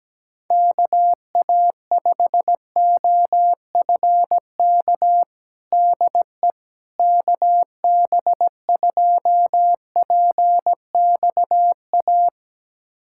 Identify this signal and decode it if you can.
Morse code, heard as a keyed tone, about 17 wpm, 700 Hz. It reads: KA5OFK DE KB2PXA
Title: USA Callsign DE Callsign - 6 Characters - 17wpm